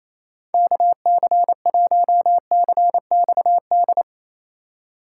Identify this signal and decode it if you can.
Morse code, heard as a keyed tone, about 28 wpm, 700 Hz. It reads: KC1CXD
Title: USA Call Signs - 6 Characters - 4x Character Spacing - 28wpm